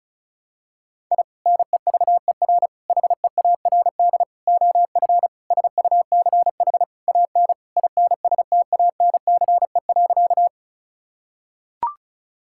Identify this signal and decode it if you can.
Morse code, heard as a keyed tone, about 36 wpm, 690 Hz. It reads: I NEVER HEARD OF SUCH AN INSTANCE.
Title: Sentences from Top 1000 Words - Review Part 5 - 35wpm